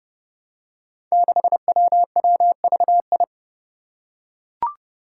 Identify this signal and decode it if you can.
Morse code, heard as a keyed tone, about 31 wpm, 700 Hz. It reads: 6WWVI
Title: Five Challenging Letter-Number Combinations 30wpm